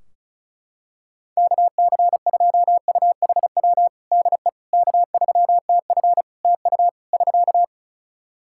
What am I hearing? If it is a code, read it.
Morse code, heard as a keyed tone, about 34 wpm, 700 Hz. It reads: KC2UHW DE K3TF TU <SK>
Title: Practice QSO - Exchange 4 - Ending - 35wpm